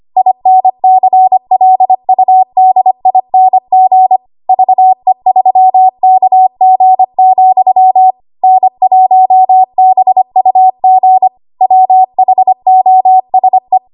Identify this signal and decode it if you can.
Morse code, heard as a keyed tone, about 25 wpm, 750 Hz. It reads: INCLUDING VE3KG, N1BUG WHOSE
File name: Morse / 25WPM